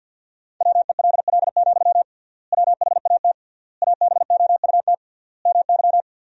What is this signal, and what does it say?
Morse code, heard as a keyed tone, about 50 wpm, 700 Hz. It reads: WELL, WHAT ABOUT MY
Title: 3x Word-Spacing - Sentences from Top 500 Words - 50wpm